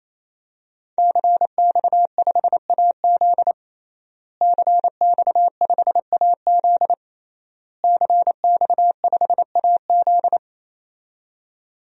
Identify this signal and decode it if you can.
Morse code, heard as a keyed tone, about 28 wpm, 700 Hz. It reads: CX5AZ CX5AZ CX5AZ
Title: DXCC-100 Call Signs - 5 Characters - Repeated 3x - 28wpm